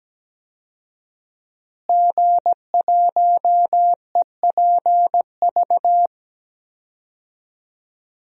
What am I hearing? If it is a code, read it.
Morse code, heard as a keyed tone, about 17 wpm, 700 Hz. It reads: G1EPV
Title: DXCC-100 Call Signs - 5 Characters - Repeated 2x - 17wpm